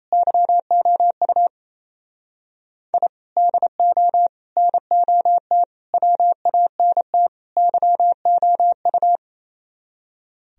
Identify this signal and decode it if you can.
Morse code, heard as a keyed tone, about 36 wpm, 700 Hz. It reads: YOU I DO NOT WANT YOU